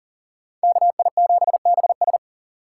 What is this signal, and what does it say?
Morse code, heard as a keyed tone, about 39 wpm, 700 Hz. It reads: KI7BS
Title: USA Call Signs - 5 Characters - 3x Character Spacing - 40wpm